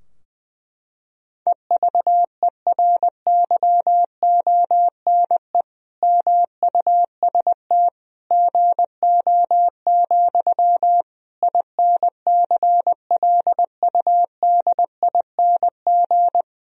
Morse code, heard as a keyed tone, about 20 wpm, 700 Hz. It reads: EVERYONE MUST GO, INCLUDING
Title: Sentences from Top 1000 Words - Review Part 4 - 20wpm